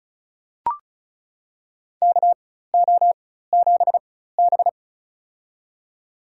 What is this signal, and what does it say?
Morse code, heard as a keyed tone, about 36 wpm, spaced as for 16 wpm, 700 Hz. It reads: KO7B
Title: USA Call Signs - 4 Characters - 4x Character Spacing - 35wpm